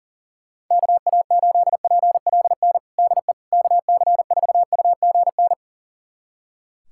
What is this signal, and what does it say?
Morse code, heard as a keyed tone, about 40 wpm, 700 Hz. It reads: KA8PLN DE KC4UGN
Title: USA Callsign DE Callsign 40wpm